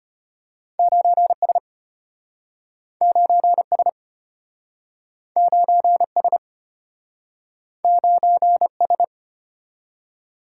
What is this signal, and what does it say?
Morse code, heard as a keyed tone, about 34 wpm, 700 Hz. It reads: 9S 9S 9S 9S